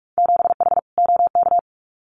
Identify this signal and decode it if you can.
Morse code, heard as a keyed tone, about 46 wpm, 690 Hz. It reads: 7H OK
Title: Winter Field Day Exchanges Encore 45wpm